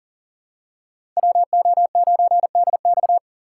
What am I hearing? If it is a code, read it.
Morse code, heard as a keyed tone, about 43 wpm, 700 Hz. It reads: WO9DX
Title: USA Call Signs - 5 Characters - 6x Character Spacing - 40wpm